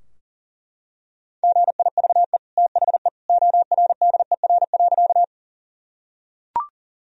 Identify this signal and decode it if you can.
Morse code, heard as a keyed tone, about 40 wpm, 700 Hz. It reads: GIVE THE ORDER.
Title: Sentences from Top 1000 Words - Review Part 2 - 40wpm